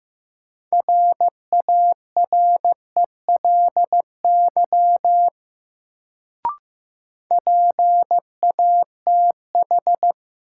Morse code, heard as a keyed tone, about 15 wpm, 700 Hz. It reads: RARELY PATH
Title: Sentences from Top 1400 Words - New Word Drill - 15wpm